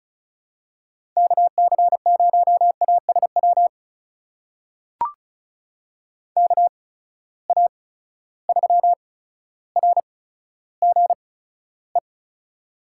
Morse code, heard as a keyed tone, about 36 wpm, spaced as for 15 wpm, 700 Hz. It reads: KC0ASW KA3RGE
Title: USA Call Signs - 8x Character Spacing - 35wpm